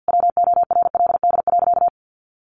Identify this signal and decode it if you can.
Morse code, heard as a keyed tone, about 50 wpm, 700 Hz. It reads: WORLD.
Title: Sentences from Top 200 Words 50wpm